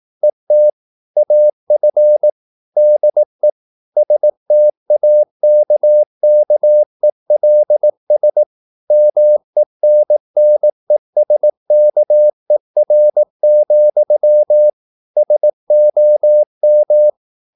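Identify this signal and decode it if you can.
Morse code, heard as a keyed tone, about 18 wpm, 600 Hz. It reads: ET AF DE STAKKELS MENNESKER, SOM
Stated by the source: Slow (QRS) Morse practice at 18 wpm, CW med Gnister.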